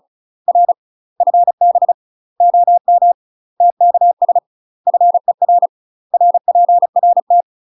Morse code, heard as a keyed tone, about 35 wpm, 700 Hz. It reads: R FB OM TKS FER RPRT
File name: UK MASTER QSOs TRAINING / 35WPM / UK MASTER QSOs 0015